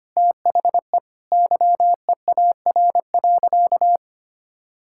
Morse code, heard as a keyed tone, about 25 wpm, 700 Hz. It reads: THE YEAR.
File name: practice / 25wpm / Sentences with 3-Word Phrases - 25wpm